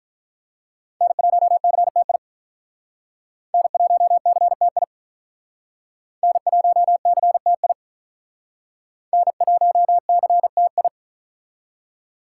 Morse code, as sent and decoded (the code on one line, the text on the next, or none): -. .---- -.-. - .. / -. .---- -.-. - .. / -. .---- -.-. - .. / -. .---- -.-. - ..
N1CTI N1CTI N1CTI N1CTI